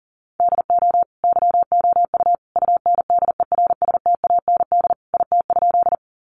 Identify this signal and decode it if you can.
Morse code, heard as a keyed tone, about 40 wpm, 700 Hz. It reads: DO YOU UNDERSTAND IT?
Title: Sentences from Top 600 Words 40wpm